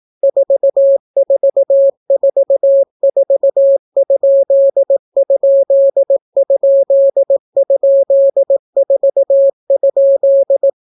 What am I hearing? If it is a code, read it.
Morse code, heard as a keyed tone, about 18 wpm, 550 Hz. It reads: 4444????4?